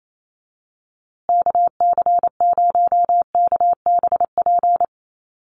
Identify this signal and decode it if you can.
Morse code, heard as a keyed tone, about 28 wpm, 700 Hz. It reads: KC0KBP
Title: USA Call Signs Repeated 3x 28wpm